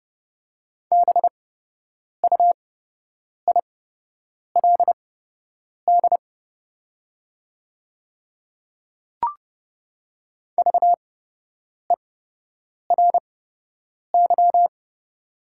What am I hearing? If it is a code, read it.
Morse code, heard as a keyed tone, about 30 wpm, spaced as for 8 wpm, 700 Hz. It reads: BUILD VERY
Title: 8x Character Spacing - Top 200 Words - 30wpm